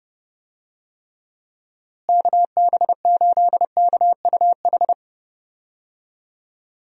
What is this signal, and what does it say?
Morse code, heard as a keyed tone, about 30 wpm, 700 Hz. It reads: KB8KUH
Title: USA Call Signs - 6 Characters - Repeated 2x - 30wpm